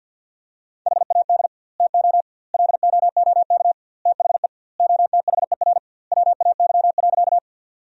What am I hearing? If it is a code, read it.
Morse code, heard as a keyed tone, about 50 wpm, 700 Hz. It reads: HAD TO LOOK THE OTHER WAY.
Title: 2x Word-Spacing - Sentences from Top 100 Words - 50wpm